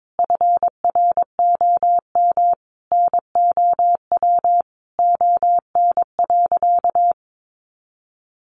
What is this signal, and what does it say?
Morse code, heard as a keyed tone, about 22 wpm, 700 Hz. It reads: FROM NOW ON.
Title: Easy Phrases 22wpm